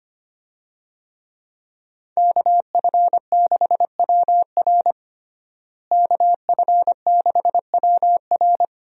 Morse code, heard as a keyed tone, about 25 wpm, 700 Hz. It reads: KF6WR KF6WR
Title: USA Call Signs - 5 Characters - Repeated 2x - 25wpm